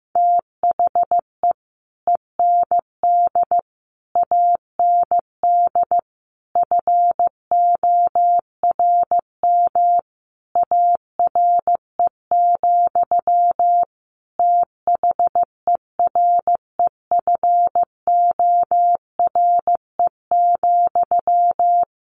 Morse code, heard as a keyed tone, about 15 wpm, 700 Hz. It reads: THE END AND FORM ARE, THEREFORE,
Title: Sentences from Top 1400 Words 15wpm